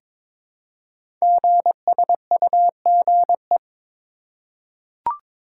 Morse code, heard as a keyed tone, about 22 wpm, 700 Hz. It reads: GSUGE
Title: Code Groups - 5 Letters - 22wpm